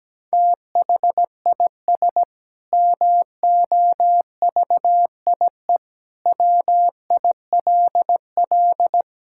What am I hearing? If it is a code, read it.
Morse code, heard as a keyed tone, about 17 wpm, 700 Hz. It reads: THIS MOVIE WILL
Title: Sentences from Top 1000 Words - Review Part 5 - 17wpm